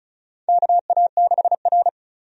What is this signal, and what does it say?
Morse code, heard as a keyed tone, about 36 wpm, 700 Hz. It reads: KA6R
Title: USA Call Signs - 4 Characters - 35wpm